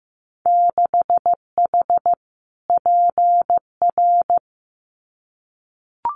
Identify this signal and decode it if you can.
Morse code, heard as a keyed tone, about 15 wpm, 700 Hz. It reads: 6H PR
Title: Winter Field Day Exchanges Encore 15wpm